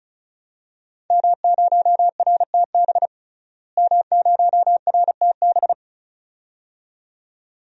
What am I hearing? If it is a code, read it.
Morse code, heard as a keyed tone, about 36 wpm, 700 Hz. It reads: M0RTB M0RTB
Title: DXCC-100 Call Signs - 5 Characters - Repeated 2x - 35wpm